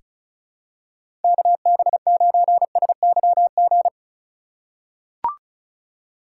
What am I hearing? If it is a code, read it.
Morse code, heard as a keyed tone, about 36 wpm, 700 Hz. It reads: KB9SYG
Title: USA Call Signs - 6 Characters - 4x Character Spacing - 35wpm